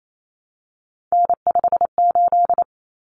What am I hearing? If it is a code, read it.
Morse code, heard as a keyed tone, about 28 wpm, 700 Hz. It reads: N58
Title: Common Cut Numbers - 3-Digit 28wpm